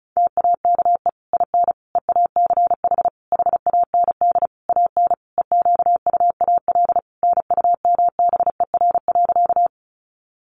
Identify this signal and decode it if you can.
Morse code, heard as a keyed tone, about 34 wpm, 700 Hz. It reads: TAKE IN EACH HAND AN EQUAL NUMBER.
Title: Sentences from Top 1400 Words 35wpm